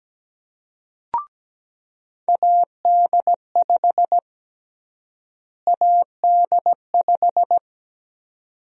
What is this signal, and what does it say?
Morse code, heard as a keyed tone, about 17 wpm, 700 Hz. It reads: AD5 AD5
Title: USA Call Sign Prefixes - Repeated 2x - 17wpm